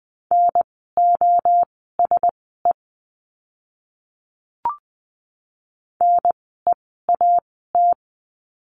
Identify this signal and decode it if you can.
Morse code, heard as a keyed tone, about 20 wpm, spaced as for 14 wpm, 700 Hz. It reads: NOSE NEAT
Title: 036 Lesson 7f - Letter s - Words 20wpm